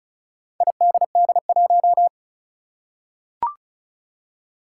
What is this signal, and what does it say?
Morse code, heard as a keyed tone, about 36 wpm, 700 Hz. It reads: IDD1
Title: Four Challenging Letter-Number Combinations 35wpm